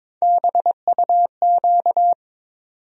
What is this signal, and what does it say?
Morse code, heard as a keyed tone, about 21 wpm, 700 Hz. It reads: BUQ
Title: USA Call Sign Postfixes - 3 Letters 22wpm